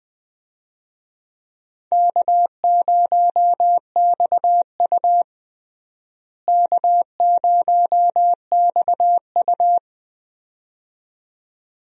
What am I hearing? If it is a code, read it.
Morse code, heard as a keyed tone, about 20 wpm, 700 Hz. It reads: K0XU K0XU
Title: USA Call Signs - 4 Characters - Repeated 2x - 20wpm